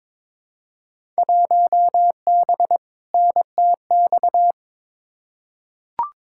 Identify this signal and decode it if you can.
Morse code, heard as a keyed tone, about 22 wpm, 700 Hz. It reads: 1B NTX
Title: Field Day Exchanges - 22wpm